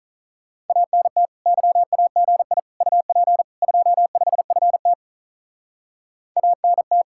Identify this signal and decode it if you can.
Morse code, heard as a keyed tone, about 43 wpm, 700 Hz. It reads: ANT YAGI UP 25FT ANT
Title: Speed-Racing Common QSO Phrases - 35wpm